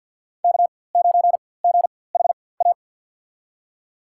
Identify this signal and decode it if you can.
Morse code, heard as a keyed tone, about 51 wpm, spaced as for 23 wpm, 700 Hz. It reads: K9GHA
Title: USA Call Signs - 5 Characters - 4x Character Spacing - 50wpm